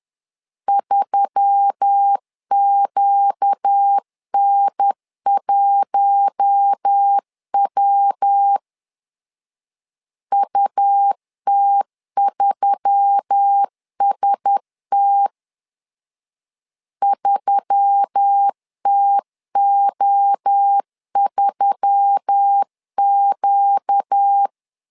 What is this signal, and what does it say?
Morse code, heard as a keyed tone, about 10 wpm, 800 Hz. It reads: 3QN1W UT3ST 3TO3Q
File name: LEZIONI / 4 Q1S3 10wpm 2427